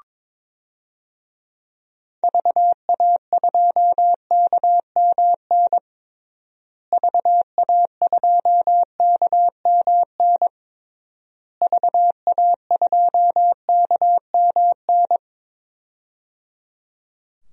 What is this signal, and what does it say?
Morse code, heard as a keyed tone, about 21 wpm, 700 Hz. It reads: VA2KMN VA2KMN VA2KMN
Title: DXCC-100 Call Signs - 6 Characters - Repeated 3x - 22wpm